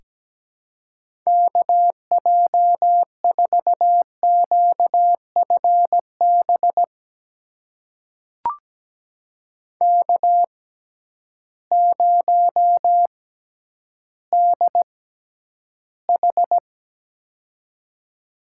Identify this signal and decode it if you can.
Morse code, heard as a keyed tone, about 17 wpm, spaced as for 10 wpm, 700 Hz. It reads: KJ4QFB K0DH
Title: USA Call Signs - 6x Character Spacing - 17wpm